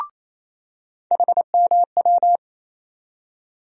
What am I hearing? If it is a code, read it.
Morse code, heard as a keyed tone, about 28 wpm, 700 Hz. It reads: HMW
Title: USA Call Sign Postfixes - 3 Letters 28wpm